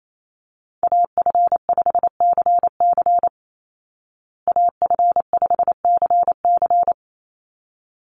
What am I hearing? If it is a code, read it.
Morse code, heard as a keyed tone, about 28 wpm, 700 Hz. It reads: AF5CC AF5CC
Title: Call Signs Repeated 3x - 28wpm